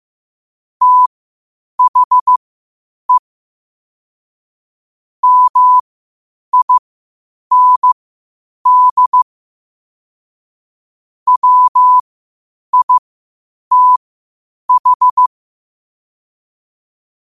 Morse code, heard as a keyed tone, about 15 wpm, spaced as for 8 wpm, 1000 Hz. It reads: THE MIND WITH
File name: qotdfiles / 8wpm